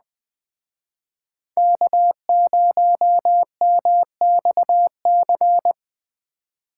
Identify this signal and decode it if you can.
Morse code, heard as a keyed tone, about 20 wpm, 700 Hz. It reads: K0MXC
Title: Speed Racing - USA 1x3 Call Signs 17wpm